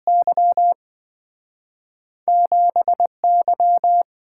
Morse code, heard as a keyed tone, about 27 wpm, spaced as for 24 wpm, 700 Hz. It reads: Y 7Y
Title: Speed-Racing - Two Letter-Number Combinations - 20wpm